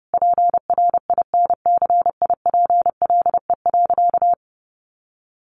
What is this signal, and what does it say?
Morse code, heard as a keyed tone, about 30 wpm, 700 Hz. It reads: PRINCIPLE.
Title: Sentences from Top 1400 Words 30wpm